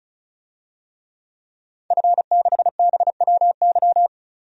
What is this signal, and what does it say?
Morse code, heard as a keyed tone, about 36 wpm, 700 Hz. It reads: F6BWY
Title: DXCC-100 Call Signs - 5 Characters - 35wpm